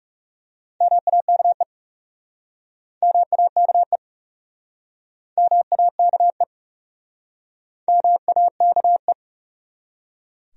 Morse code, heard as a keyed tone, about 37 wpm, 700 Hz. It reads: MAKE MAKE MAKE MAKE
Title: Speed-Racing Top 200 Words 30wpm